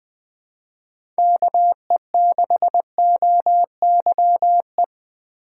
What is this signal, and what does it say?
Morse code, heard as a keyed tone, about 20 wpm, 700 Hz. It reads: KE6OYE
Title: USA Call Signs - 2x Character Spacing - 20wpm